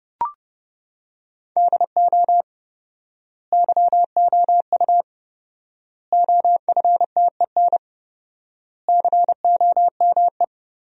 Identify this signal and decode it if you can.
Morse code, heard as a keyed tone, about 32 wpm, 700 Hz. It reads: DO YOU OFTEN COME
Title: 4x Word-Spacing - Sentences from Top 300 Words - 30wpm